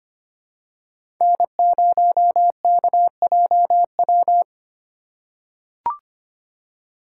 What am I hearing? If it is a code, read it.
Morse code, heard as a keyed tone, about 25 wpm, 700 Hz. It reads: N0KJW
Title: USA Call Signs - 5 Characters - 2x Character Spacing - 25wpm